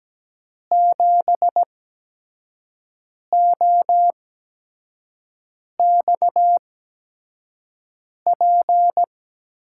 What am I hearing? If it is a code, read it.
Morse code, heard as a keyed tone, about 17 wpm, spaced as for 5 wpm, 700 Hz. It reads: 7OXP
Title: USA Call Signs - 6 Characters - 8x Character Spacing - 17wpm